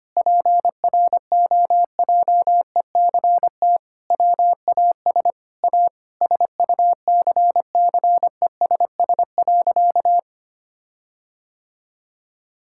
Morse code, heard as a keyed tone, about 25 wpm, 700 Hz. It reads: PROJECT WAS A SUCCESS.
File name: practice / 25wpm / Sentences with 9-Word Phrases - 25wpm